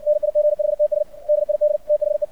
Morse code, heard as a keyed tone, about 64 wpm, 600 Hz. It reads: NIGHTS DID THE
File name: archived / CRSnet